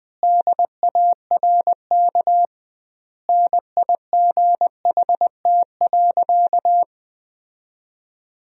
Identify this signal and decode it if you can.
Morse code, heard as a keyed tone, about 21 wpm, 700 Hz. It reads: DARK NIGHT.
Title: 2x Word-Spacing - Sentences from Top 500 Words - 20wpm